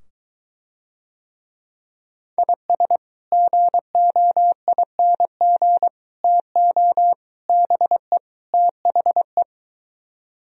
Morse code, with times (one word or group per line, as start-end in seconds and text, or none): 2.4-3.0 IS
3.3-5.9 GOING
6.2-7.1 TO
7.5-8.2 BE
8.5-9.4 THE